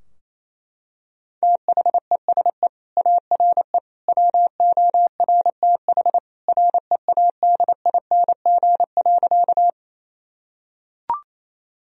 Morse code, heard as a keyed tone, about 27 wpm, 690 Hz. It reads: THESE ARE WORTH READING.
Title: Sentences from Top 1000 Words - Review Part 3 - 28wpm